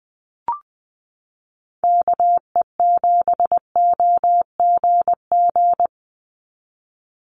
Morse code, heard as a keyed tone, about 20 wpm, 700 Hz. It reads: KE7OGG
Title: USA Call Signs 20wpm